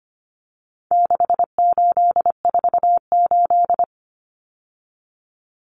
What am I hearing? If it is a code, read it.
Morse code, heard as a keyed tone, about 25 wpm, 700 Hz. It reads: 6848
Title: Numbers - 4-Digit 25wpm